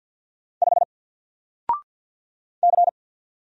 Morse code, heard as a keyed tone, about 50 wpm, 700 Hz. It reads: 5 C
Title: Single Letter-Number Alt 50wpm